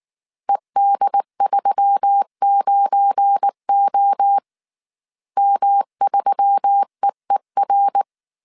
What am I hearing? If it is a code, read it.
Morse code, heard as a keyed tone, about 19 wpm, 800 Hz. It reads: ED39O M3EER
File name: CW / DET / 1 9 1500C S A 18wpm 2459